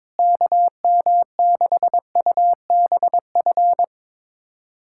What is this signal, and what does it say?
Morse code, heard as a keyed tone, about 22 wpm, 700 Hz. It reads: KM6UBF
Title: USA Callsign DE Callsign 22wpm